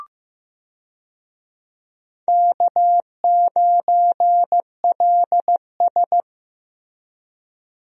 Morse code, heard as a keyed tone, about 15 wpm, 700 Hz. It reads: K9LS
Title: USA Call Signs - 4 Characters - Repeated 2x - 15wpm